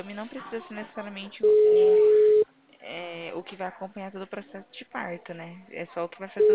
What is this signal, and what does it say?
Ringback tone (Europe, 425 Hz), with speech.